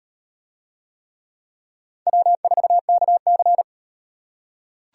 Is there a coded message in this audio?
Yes: Morse code from 2.1 to 3.6 s, heard as a keyed tone, about 36 wpm, 700 Hz. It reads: W4KC